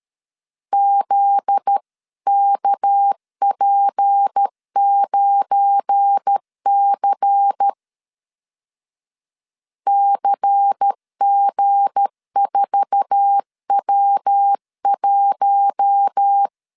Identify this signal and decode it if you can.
Morse code, heard as a keyed tone, about 13 wpm, 800 Hz. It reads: ZKP9C CG4W1
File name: CW / DET / 1 9 1500C S B 12wpm 3819